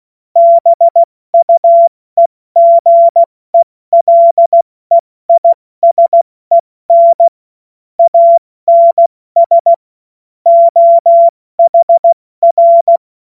Morse code, heard as a keyed tone, about 16 wpm, 680 Hz. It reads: BUEGELEISEN ANS OHR
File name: Witz-09-September-2025-70-F80